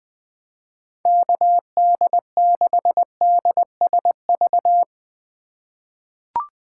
Morse code, heard as a keyed tone, about 20 wpm, 700 Hz. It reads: KD6DSV
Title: USA Call Signs - 6 Characters - 3x Character Spacing - 20wpm